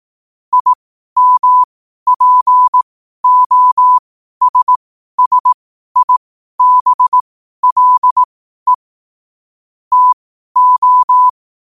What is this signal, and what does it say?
Morse code, heard as a keyed tone, about 18 wpm, spaced as for 12 wpm, 1000 Hz. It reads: IMPOSSIBLE TO
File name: qotdfiles / 12wpm